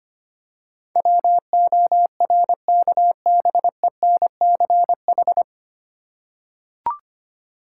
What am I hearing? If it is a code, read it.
Morse code, heard as a keyed tone, about 25 wpm, 700 Hz. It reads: WORKBENCH
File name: practice / 25wpm / Difficult Words with Hints - 25wpm